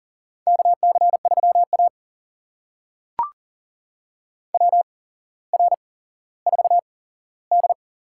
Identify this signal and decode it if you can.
Morse code, heard as a keyed tone, about 38 wpm, spaced as for 17 wpm, 700 Hz. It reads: KC3A WR4D
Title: USA Call Signs - 4 Characters - 8x Character Spacing - 40wpm